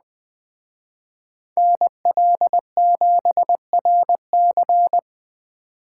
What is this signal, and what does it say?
Morse code, heard as a keyed tone, about 20 wpm, 700 Hz. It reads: NL7RC
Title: Speed Racing - USA 2x2 Call Signs 17wpm